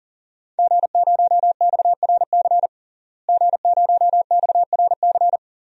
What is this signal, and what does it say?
Morse code, heard as a keyed tone, about 40 wpm, 700 Hz. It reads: G0XRC G0XRC
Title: DXCC-100 Call Signs - 5 Characters - Repeated 2x - 40wpm